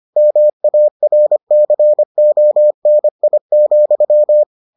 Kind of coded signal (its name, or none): Morse code